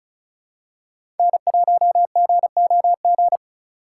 Morse code, heard as a keyed tone, about 36 wpm, 700 Hz. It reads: N1GOG
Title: USA Call Signs - 5 Characters - 4x Character Spacing - 35wpm